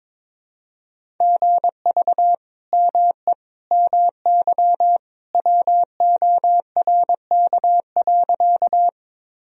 Morse code, heard as a keyed tone, about 22 wpm, 700 Hz. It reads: GV ME MY WORK.
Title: Sentences from Top 200 Words and Abbreviations 22wpm